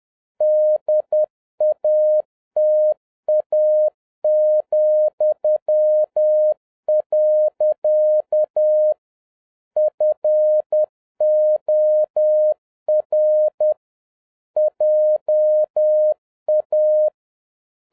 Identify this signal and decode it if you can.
Morse code, heard as a keyed tone, about 10 wpm, 610 Hz. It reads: DATA,. FOR JA
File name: 2011-07-mp3 / n21 GREETINGS FROM SINGAPORE 10wpm